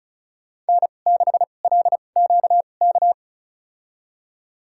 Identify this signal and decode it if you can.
Morse code, heard as a keyed tone, about 36 wpm, spaced as for 25 wpm, 700 Hz. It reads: N6LQK
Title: USA Call Signs - 5 Characters - 2x Character Spacing - 35wpm